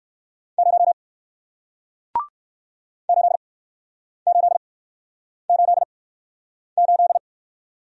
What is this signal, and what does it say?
Morse code, heard as a keyed tone, about 59 wpm, spaced as for 9 wpm, 700 Hz. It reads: 0 8888